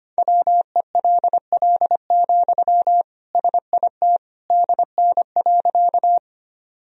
Morse code, heard as a keyed tone, about 25 wpm, 700 Hz. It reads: WELL, SIT DN.
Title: Sentences from Top 500 Words and Abbreviations 25wpm